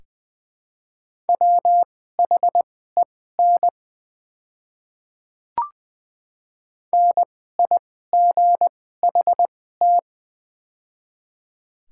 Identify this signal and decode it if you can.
Morse code, heard as a keyed tone, about 20 wpm, spaced as for 14 wpm, 700 Hz. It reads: WHEN NIGHT
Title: 2x Character Spacing - Top 300 Words - 20wpm